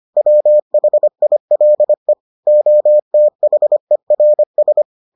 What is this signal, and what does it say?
Morse code, heard as a keyed tone, about 25 wpm, 600 Hz. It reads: WHILE OTHERS